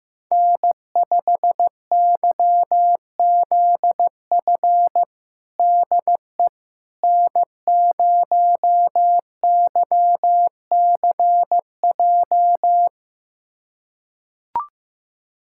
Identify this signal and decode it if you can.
Morse code, heard as a keyed tone, about 15 wpm, 700 Hz. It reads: N5YZF DE N0YCJ
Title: USA Callsign DE Callsign - 5 Characters - 15wpm